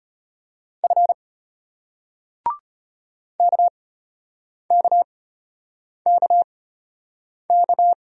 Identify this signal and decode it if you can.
Morse code, heard as a keyed tone, about 34 wpm, spaced as for 8 wpm, 700 Hz. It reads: F KKKK